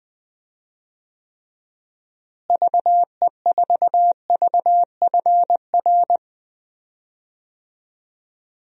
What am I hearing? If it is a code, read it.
Morse code, heard as a keyed tone, about 20 wpm, 700 Hz. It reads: VE4VFR
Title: DXCC-100 Call Signs - 6 Characters - Repeated 2x - 20wpm